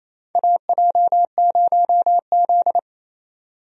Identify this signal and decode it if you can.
Morse code, heard as a keyed tone, about 28 wpm, 700 Hz. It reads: AJ0Z
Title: USA Call Signs - 4 Characters - 28wpm